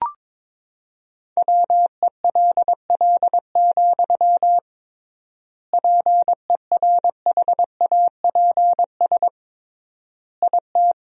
Morse code, heard as a keyed tone, about 22 wpm, 700 Hz. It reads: WELL, PERHAPS IT
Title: 3x Word-Spacing - Sentences from Top 500 Words - 22wpm